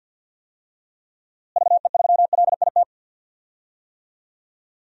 Morse code, heard as a keyed tone, about 52 wpm, 700 Hz. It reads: VE3LIT
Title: DXCC-100 Call Signs - 6 Characters - Repeated 3x - 50wpm